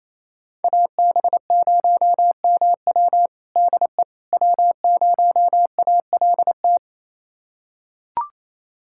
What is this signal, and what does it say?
Morse code, heard as a keyed tone, about 28 wpm, 700 Hz. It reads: AB0MW DE W0ALT
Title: USA Callsign DE Callsign - 5 Characters - 28wpm